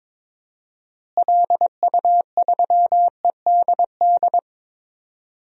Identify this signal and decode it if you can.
Morse code, heard as a keyed tone, about 22 wpm, 700 Hz. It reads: LU3EDD
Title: DXCC-100 Call Signs - 6 Characters - Repeated 2x - 22wpm